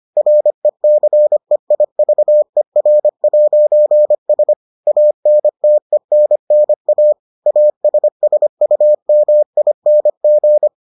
Morse code, heard as a keyed tone, about 25 wpm, 600 Hz. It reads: RECEIVER'S ANTENNA ASSUMING